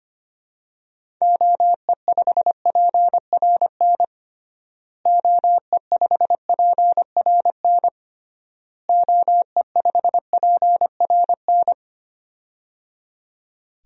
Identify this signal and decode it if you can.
Morse code, heard as a keyed tone, about 25 wpm, 700 Hz. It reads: OE5PRN OE5PRN OE5PRN
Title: DXCC-100 Call Signs - 6 Characters - Repeated 3x - 25wpm